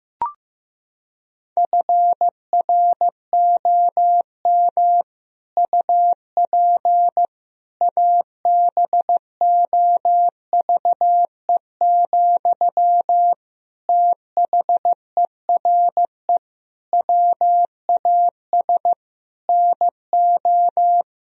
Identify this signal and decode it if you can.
Morse code, heard as a keyed tone, about 16 wpm, 700 Hz. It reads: FROM UP ABOVE, THERE WAS NO
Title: Sentences from Top 1000 Words - Review Part 2 - 15wpm